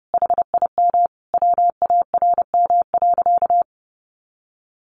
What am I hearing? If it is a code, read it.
Morse code, heard as a keyed tone, about 30 wpm, 700 Hz. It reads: HIM WARM.
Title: Sentences from Top 1500 Words 30wpm